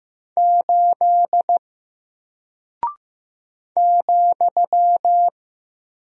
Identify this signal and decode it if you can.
Morse code, heard as a keyed tone, about 15 wpm, 700 Hz. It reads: 8 ,
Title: Single Character (Letter:Number:Punctuation:Prosign) 15wpm